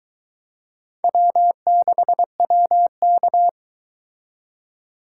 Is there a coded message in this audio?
Yes: Morse code from 1.0 to 3.5 s, heard as a keyed tone, about 22 wpm, 700 Hz. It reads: W6WK